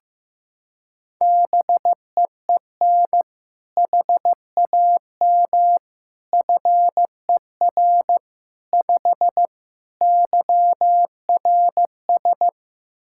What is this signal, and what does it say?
Morse code, heard as a keyed tone, about 15 wpm, 700 Hz. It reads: BEEN HAM FER 5 YRS
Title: K1USN SST QSO Practice 15wpm